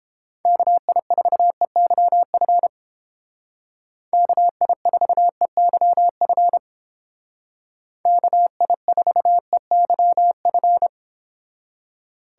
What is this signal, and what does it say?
Morse code, heard as a keyed tone, about 28 wpm, 700 Hz. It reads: KI4EYF KI4EYF KI4EYF